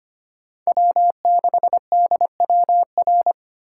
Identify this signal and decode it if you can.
Morse code, heard as a keyed tone, about 25 wpm, 700 Hz. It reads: W6DWR
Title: USA Call Signs - 5 Characters - 2x Character Spacing - 25wpm